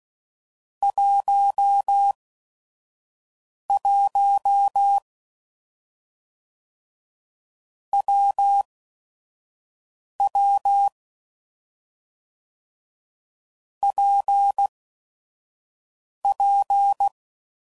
Morse code, heard as a keyed tone, about 16 wpm, spaced as for 5 wpm, 800 Hz. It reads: 11 WW PP
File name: Combinations / Difficult Combos / AWJ1